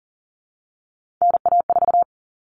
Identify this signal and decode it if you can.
Morse code, heard as a keyed tone, about 40 wpm, 700 Hz. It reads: NA4
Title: All Cut Numbers - 3-Digit 40wpm